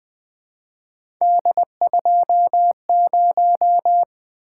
Morse code, heard as a keyed tone, about 20 wpm, 700 Hz. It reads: D20
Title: Three Letter-Number Combinations - 20wpm